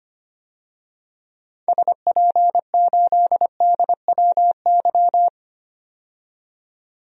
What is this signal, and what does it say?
Morse code, heard as a keyed tone, about 25 wpm, 700 Hz. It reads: SP8DWY
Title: DXCC-100 Call Signs - 6 Characters - 25wpm